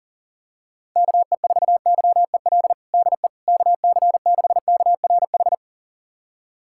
Morse code, heard as a keyed tone, about 40 wpm, 700 Hz. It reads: KE4YEL DE KC6KRH
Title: USA Callsign DE Callsign 40wpm